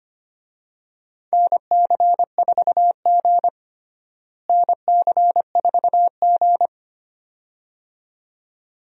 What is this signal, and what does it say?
Morse code, heard as a keyed tone, about 25 wpm, 700 Hz. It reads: NC4G NC4G
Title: USA Call Signs - 4 Characters - Repeated 2x - 25wpm